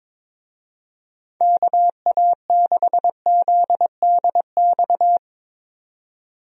Morse code, heard as a keyed tone, about 22 wpm, 700 Hz. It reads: KA6ZDX
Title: USA Call Signs - 6 Characters - 22wpm